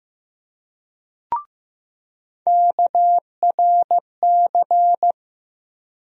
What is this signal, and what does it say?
Morse code, heard as a keyed tone, about 15 wpm, 700 Hz. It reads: KRC
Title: USA Call Sign Postfixes - 3 Letters 15wpm